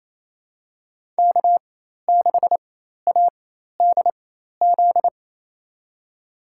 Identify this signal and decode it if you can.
Morse code, heard as a keyed tone, about 27 wpm, spaced as for 13 wpm, 700 Hz. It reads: K6ADZ
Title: USA Call Signs - 5 Characters - 4x Character Spacing - 28wpm